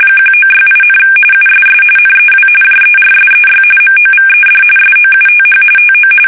RTTY, about 45.45 baud. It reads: 1968, USS NEW JERSEY MOVED SOUTH TO
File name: fsk850 / battleshipnj